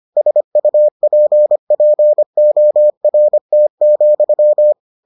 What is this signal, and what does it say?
Morse code, heard as a keyed tone, about 25 wpm, 600 Hz. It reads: SUPPORT,